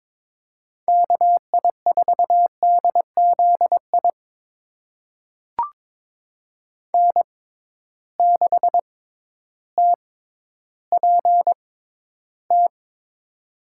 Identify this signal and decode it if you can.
Morse code, heard as a keyed tone, about 22 wpm, spaced as for 12 wpm, 700 Hz. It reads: KI4DZI N6TPT
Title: USA Call Signs - 6x Character Spacing - 22wpm